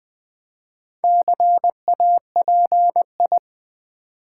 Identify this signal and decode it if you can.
Morse code, heard as a keyed tone, about 20 wpm, 700 Hz. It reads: CAPI
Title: Build A Word - 4 Letters At A Time - 10-Letter Words - 20wpm